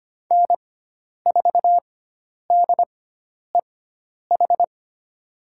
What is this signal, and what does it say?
Morse code, heard as a keyed tone, about 25 wpm, spaced as for 10 wpm, 700 Hz. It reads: N4DEH
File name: Lesson 16g - Letter c - Call Signs 25wpm